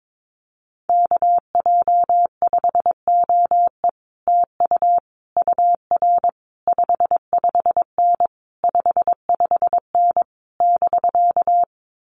Morse code, heard as a keyed tone, about 22 wpm, 700 Hz. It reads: KJ5OE TU UR 55N 55N <BK>
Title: Common QSO Phrases Encore 22wpm